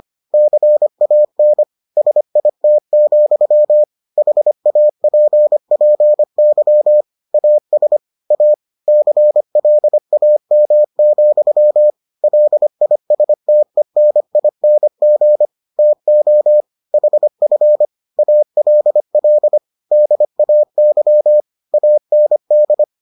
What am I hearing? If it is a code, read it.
Morse code, heard as a keyed tone, about 25 wpm, 600 Hz. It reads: CAN SIT, HAPPY AS A CLAM, LISTENING TO HF ALL DAY AND